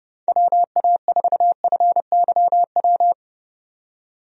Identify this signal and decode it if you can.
Morse code, heard as a keyed tone, about 31 wpm, 700 Hz. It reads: WA4FYW
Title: USA Call Signs - 6 Characters - 30wpm